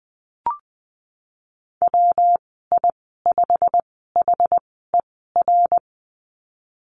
Morse code, heard as a keyed tone, about 20 wpm, spaced as for 15 wpm, 700 Hz. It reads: WI5HER
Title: 193 Lesson 31g - Letter v - Call Signs 20wpm